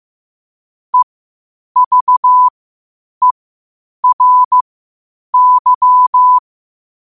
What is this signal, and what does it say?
Morse code, heard as a keyed tone, about 15 wpm, spaced as for 8 wpm, 1000 Hz. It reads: EVERY
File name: qotdfiles / 8wpm